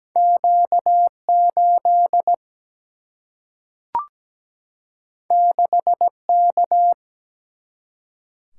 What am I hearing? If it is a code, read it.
Morse code, heard as a keyed tone, about 17 wpm, 700 Hz. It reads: Q8 6K
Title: Single-Letter Single-Number 17wpm